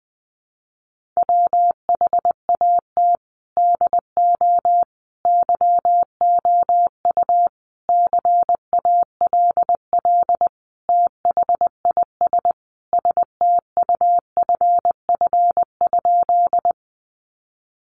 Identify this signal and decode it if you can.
Morse code, heard as a keyed tone, about 20 wpm, 700 Hz. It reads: WHAT DO YOU CALL THIS STUFF?
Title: Sentences from Top 1400 Words 20wpm